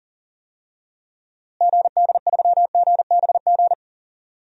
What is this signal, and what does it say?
Morse code, heard as a keyed tone, about 40 wpm, 700 Hz. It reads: GD3GBG
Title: DXCC-100 Call Signs - 6 Characters - Repeated 3x - 40wpm